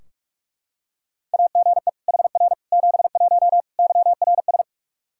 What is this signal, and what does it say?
Morse code, heard as a keyed tone, about 41 wpm, 700 Hz. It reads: AGE HR 71 YRS
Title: Practice QSO - Exchange 3 - More Info - 45wpm